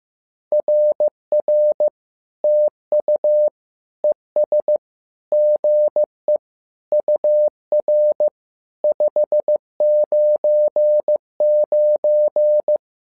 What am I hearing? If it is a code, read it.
Morse code, heard as a keyed tone, about 15 wpm, 600 Hz. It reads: RR TU ES GE UR 599
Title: SOTA QSO Practice 15wpm